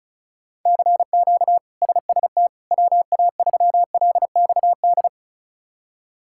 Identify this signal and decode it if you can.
Morse code, heard as a keyed tone, about 36 wpm, 700 Hz. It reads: CQ SST WA3LXD
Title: K1USN SST QSO Practice - 35wpm